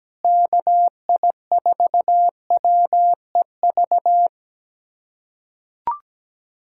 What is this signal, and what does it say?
Morse code, heard as a keyed tone, about 17 wpm, 700 Hz. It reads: KI4WEV
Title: USA Callsign DE Callsign - 6 Characters - 17wpm